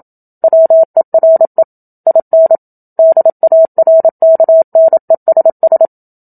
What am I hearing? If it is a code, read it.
Morse code, heard as a keyed tone, about 27 wpm, 650 Hz. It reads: WERE IN DARKNESS
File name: Book Two / 01-Chapter 1 - 27 WPM